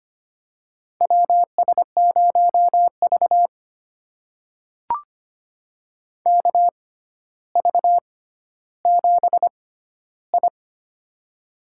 Morse code, heard as a keyed tone, about 25 wpm, spaced as for 13 wpm, 700 Hz. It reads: WS0V KV7I
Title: USA Call Signs - 4 Characters - 6x Character Spacing - 25wpm